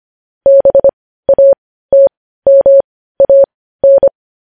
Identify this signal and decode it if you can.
Morse code, heard as a keyed tone, about 25 wpm, spaced as for 15 wpm, 550 Hz. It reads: BATMAN
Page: news headlines in morse code at 15 wpm